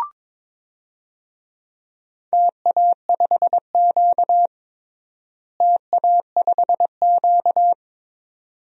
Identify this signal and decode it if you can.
Morse code, heard as a keyed tone, about 23 wpm, 700 Hz. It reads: TA5Q TA5Q
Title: DXCC-100 Call Signs - 4 Characters - Repeated 2x - 22wpm